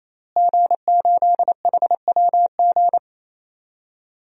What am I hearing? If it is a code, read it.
Morse code, heard as a keyed tone, about 28 wpm, 700 Hz. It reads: G8HWG
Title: DXCC-100 Call Signs - 5 Characters - Repeated 3x - 28wpm